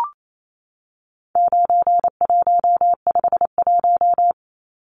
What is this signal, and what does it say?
Morse code, heard as a keyed tone, about 28 wpm, 700 Hz. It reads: 9151
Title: Numbers - 4-Digit 28wpm